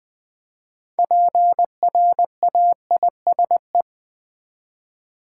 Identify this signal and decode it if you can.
Morse code, heard as a keyed tone, about 20 wpm, 700 Hz. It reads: PRAISE
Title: 6 Letter Words - First 3 Letters Given - 20wpm